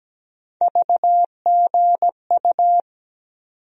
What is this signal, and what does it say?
Morse code, heard as a keyed tone, about 17 wpm, 700 Hz. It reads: VGU
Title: USA Call Sign Postfixes - 3 Letters 17wpm